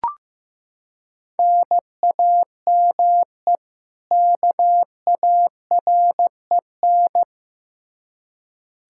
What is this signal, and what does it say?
Morse code, heard as a keyed tone, about 16 wpm, 700 Hz. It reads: NAME KAREN
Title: Practice QSO - Exchange 1 - Beginning - 15wpm